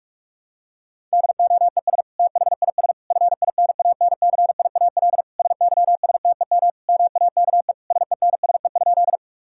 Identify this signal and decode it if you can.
Morse code, heard as a keyed tone, about 46 wpm, 700 Hz. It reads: DOES THIS FINANCIAL SYSTEM MAKE SENSE?
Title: Sentences from Top 1000 Words - Review Part 5 - 45wpm